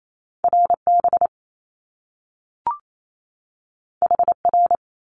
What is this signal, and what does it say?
Morse code, heard as a keyed tone, about 28 wpm, 700 Hz. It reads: RB HR
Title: Top Two Letter Combinations in Words 28wpm